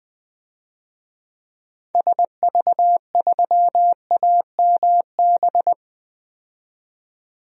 Morse code, heard as a keyed tone, about 20 wpm, 700 Hz. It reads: SV3AMB
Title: DXCC-100 Call Signs - 6 Characters - 20wpm